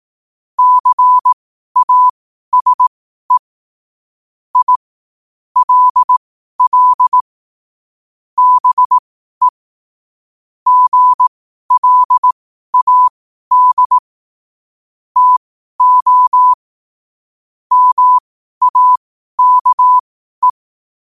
Morse code, heard as a keyed tone, about 18 wpm, spaced as for 12 wpm, 1000 Hz. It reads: CASE I LL BE GLAD TO MAKE
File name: qotdfiles / 12wpm